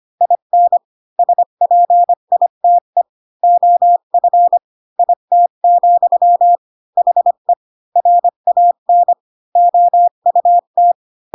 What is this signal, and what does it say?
Morse code, heard as a keyed tone, about 25 wpm, 700 Hz. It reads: IN SPITE OF IT, HE RAN OUT
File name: Pinocchio / Pinocchio ch6 20 25 ik1hge